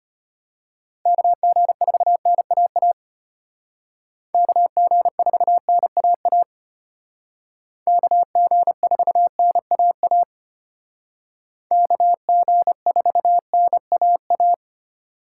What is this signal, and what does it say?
Morse code, heard as a keyed tone, about 30 wpm, 700 Hz. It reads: KG4NAA KG4NAA KG4NAA KG4NAA
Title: Speed Racing - USA 2x3 Call Signs 25wpm